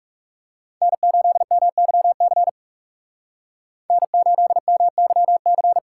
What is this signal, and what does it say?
Morse code, heard as a keyed tone, about 35 wpm, 700 Hz. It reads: N8MYC N8MYC
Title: Speed Racing - USA Call Signs 30wpm